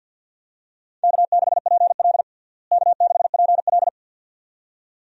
Morse code, heard as a keyed tone, about 51 wpm, 700 Hz. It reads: K6PL K6PL
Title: USA Call Signs - 4 Characters - Repeated 2x - 50wpm